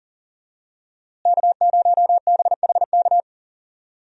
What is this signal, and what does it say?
Morse code, heard as a keyed tone, about 40 wpm, 700 Hz. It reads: K0BHK
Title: USA Call Signs - 5 Characters - Repeated 3x - 40wpm